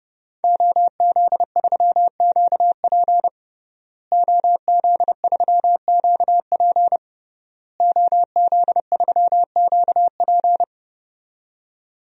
Morse code, heard as a keyed tone, about 30 wpm, 700 Hz. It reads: OZ3QP OZ3QP OZ3QP
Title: DXCC-100 Call Signs - 5 Characters - Repeated 3x - 30wpm